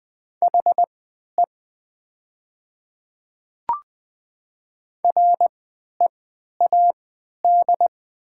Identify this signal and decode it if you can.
Morse code, heard as a keyed tone, about 20 wpm, spaced as for 11 wpm, 700 Hz. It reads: HE READ
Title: 3x Character Spacing - Top 300 Words - 20wpm